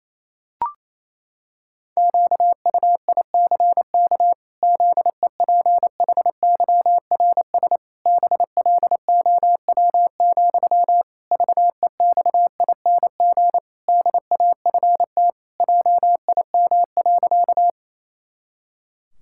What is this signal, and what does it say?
Morse code, heard as a keyed tone, about 28 wpm, 700 Hz. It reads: QUICK ZEPHYRS BLOW, VEXING DAFT JIM.
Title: Pangram Challenge v2 - 28wpm